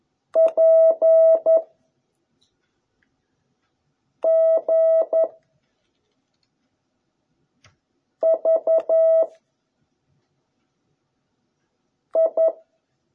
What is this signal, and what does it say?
Morse code, heard as a keyed tone, about 11 wpm, 650 Hz. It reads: P G V I